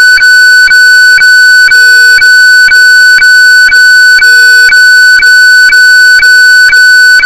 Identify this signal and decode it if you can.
Weather fax (radiofax).